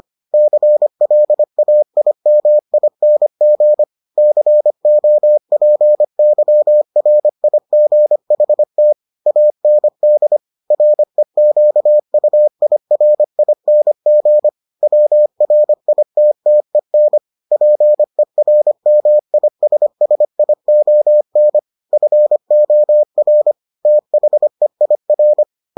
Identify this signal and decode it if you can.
Morse code, heard as a keyed tone, about 25 wpm, 600 Hz. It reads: CLAIMING COPYRIGHT AND REQUIRING WRITTEN PERMISSION FOR THEIR